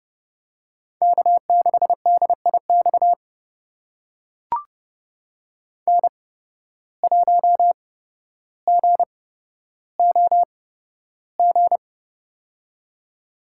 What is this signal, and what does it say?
Morse code, heard as a keyed tone, about 31 wpm, spaced as for 13 wpm, 700 Hz. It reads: K6DIX N1GOG
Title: USA Call Signs - 5 Characters - 8x Character Spacing - 30wpm